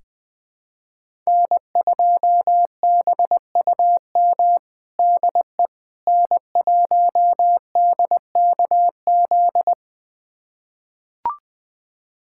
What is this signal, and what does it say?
Morse code, heard as a keyed tone, about 21 wpm, 700 Hz. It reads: N2BUM DE N1DKZ
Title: USA Callsign DE Callsign - 5 Characters - 20wpm